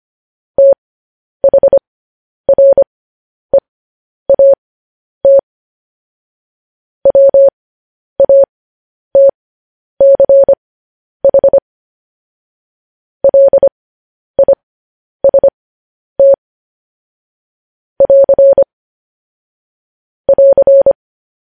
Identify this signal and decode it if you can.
Morse code, heard as a keyed tone, about 25 wpm, spaced as for 10 wpm, 550 Hz. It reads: THREAT WATCH LIST <AR> <AR>
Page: news-headlines-in-morse-code-at-10-wpm